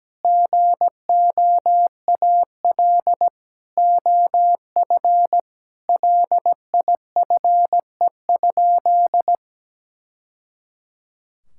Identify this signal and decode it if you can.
Morse code, heard as a keyed tone, about 17 wpm, 700 Hz. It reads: GOAL OF LIFE?
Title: Sentences from Top 1000 Words - Review Part 5 - 17wpm